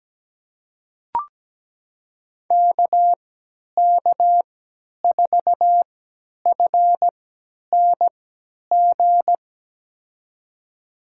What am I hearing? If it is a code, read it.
Morse code, heard as a keyed tone, about 17 wpm, spaced as for 10 wpm, 700 Hz. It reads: KK4FNG
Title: USA Call Signs - 6 Characters - 3x Character Spacing - 17wpm